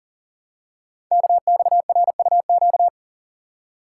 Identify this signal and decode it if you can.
Morse code, heard as a keyed tone, about 38 wpm, 690 Hz. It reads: KXRUQ
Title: Code Groups - 5 Letters - 40wpm